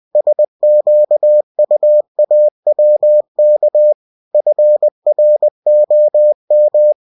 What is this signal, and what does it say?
Morse code, heard as a keyed tone, about 20 wpm, 600 Hz. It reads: SQUAWK FROM
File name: cw-books / winnie-the-pooh / winnie-20wpm-page-0101